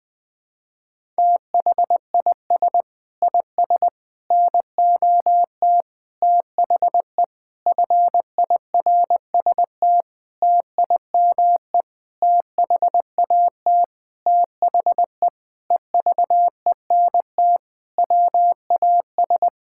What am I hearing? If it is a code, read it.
Morse code, heard as a keyed tone, about 20 wpm, 700 Hz. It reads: THIS IS NOT THE FIRST TIME THAT THE EVENT WAS
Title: Sentences with 8-Word Phrases - Spoken Beginning - 20wpm